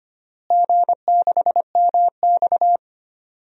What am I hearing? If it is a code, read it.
Morse code, heard as a keyed tone, about 25 wpm, 700 Hz. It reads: G6MX
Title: DXCC-100 Call Signs - 4 Characters - Repeated 3x - 25wpm